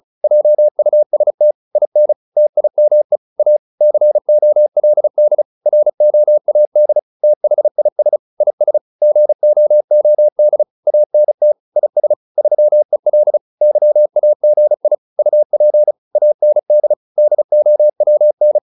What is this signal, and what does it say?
Morse code, heard as a keyed tone, about 34 wpm, 600 Hz. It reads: JUST IN TIME A COLD ROAD THIS IS GOOD ANT IS 3EL YAGI UP AND DOWN